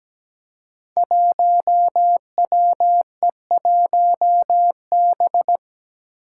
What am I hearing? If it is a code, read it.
Morse code, heard as a keyed tone, about 17 wpm, 700 Hz. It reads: 1WE1B
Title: Five Challenging Letter-Number Combinations 17wpm